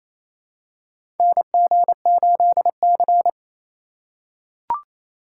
Morse code, heard as a keyed tone, about 28 wpm, 700 Hz. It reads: NG8C
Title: USA Call Signs - 4 Characters - 2x Character Spacing - 28wpm